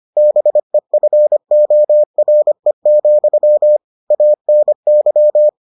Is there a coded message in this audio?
Yes: Morse code from 0.2 to 5.5 s, heard as a keyed tone, about 25 wpm, 600 Hz. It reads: BEFORE, ANY